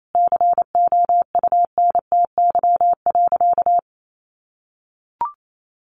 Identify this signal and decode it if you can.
Morse code, heard as a keyed tone, about 28 wpm, 700 Hz. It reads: COUNTY.
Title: Sentences from Top 1200 Words 28wpm